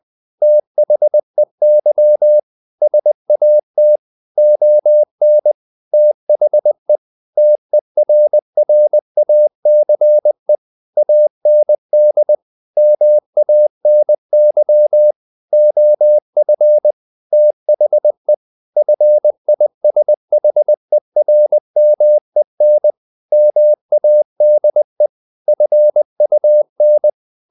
Morse code, heard as a keyed tone, about 20 wpm, 600 Hz. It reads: THEY SAT ON THE TERRACE AND MANY OF THE FISHERMEN MADE FUN
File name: the old man and the sea / the old man and the sea part 1 20wpm